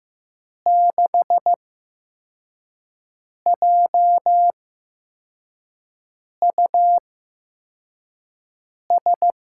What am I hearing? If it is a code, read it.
Morse code, heard as a keyed tone, about 15 wpm, spaced as for 4 wpm, 700 Hz. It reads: 6JUS
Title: USA Call Signs - 6 Characters - 8x Character Spacing - 15wpm